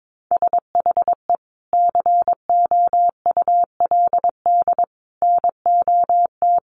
Morse code, heard as a keyed tone, about 22 wpm, 700 Hz. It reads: SHE COULD NOT
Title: Sentences from Top 1400 Words 22wpm